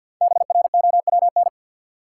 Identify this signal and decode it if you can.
Morse code, heard as a keyed tone, about 46 wpm, 700 Hz. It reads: BROWN
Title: Sentences from Top 1200 Words - New Word Drill - 50wpm